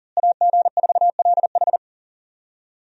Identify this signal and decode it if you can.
Morse code, heard as a keyed tone, about 40 wpm, 700 Hz. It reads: AG4LH
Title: USA Call Signs - 5 Characters - Repeated 3x - 40wpm